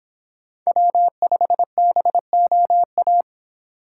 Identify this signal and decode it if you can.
Morse code, heard as a keyed tone, about 26 wpm, 700 Hz. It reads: W5BOA